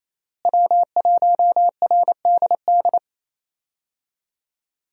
Morse code, heard as a keyed tone, about 27 wpm, 700 Hz. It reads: W1RDD
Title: USA Call Signs - 5 Characters - Repeated 3x - 28wpm